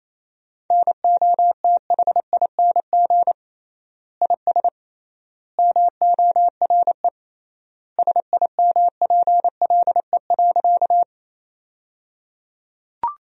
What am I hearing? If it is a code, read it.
Morse code, heard as a keyed tone, about 28 wpm, 700 Hz. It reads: NOTHING IS MORE SIMPLE.
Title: 3x Word-Spacing - Sentences from Top 500 Words - 28wpm